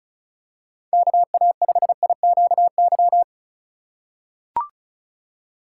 Morse code, heard as a keyed tone, about 36 wpm, 700 Hz. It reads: KA5IQY
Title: USA Call Signs - 6 Characters - 4x Character Spacing - 35wpm